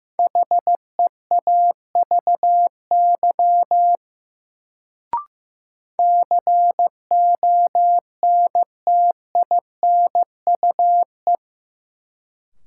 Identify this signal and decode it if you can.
Morse code, heard as a keyed tone, about 15 wpm, 700 Hz. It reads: HEAVY CONTINUE
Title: Sentences from Top 1000 Words - Review Part 6 - New Word Drill - 15wpm